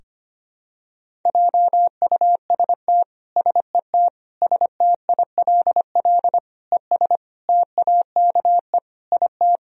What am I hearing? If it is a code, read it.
Morse code, heard as a keyed tone, about 25 wpm, 700 Hz. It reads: JUST SET STILL ES TAKE IT
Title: Sentences from Top 300 Words and Abbreviations 25wpm